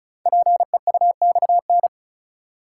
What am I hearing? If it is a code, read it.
Morse code, heard as a keyed tone, about 36 wpm, 690 Hz. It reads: PEUXN
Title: Code Groups - 5 Letters - 35wpm